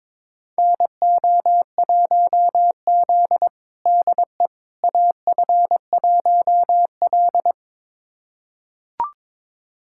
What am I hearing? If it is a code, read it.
Morse code, heard as a keyed tone, about 21 wpm, 700 Hz. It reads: NO1Z DE AF1L
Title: USA Callsign DE Callsign - 4 Characters - 22wpm